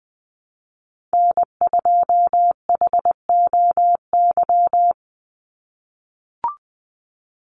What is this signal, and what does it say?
Morse code, heard as a keyed tone, about 20 wpm, 700 Hz. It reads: N2HOY
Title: USA Call Signs 20wpm